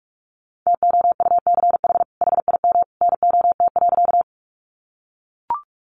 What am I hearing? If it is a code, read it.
Morse code, heard as a keyed tone, about 44 wpm, 700 Hz. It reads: TOUCH HIM NOT.
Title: Sentences from Top 1200 Words 45wpm